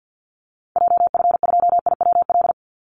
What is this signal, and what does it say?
Morse code, heard as a keyed tone, about 48 wpm, 700 Hz. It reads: JF2IWL
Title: Call Signs Repeated 3x - 50wpm